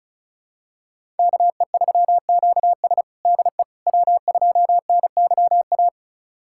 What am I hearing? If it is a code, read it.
Morse code, heard as a keyed tone, about 34 wpm, 700 Hz. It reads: KE3QS DE W2NYA
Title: USA Callsign DE Callsign - 5 Characters - 35wpm